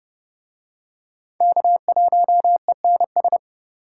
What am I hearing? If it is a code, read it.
Morse code, heard as a keyed tone, about 30 wpm, 700 Hz. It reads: K1ENS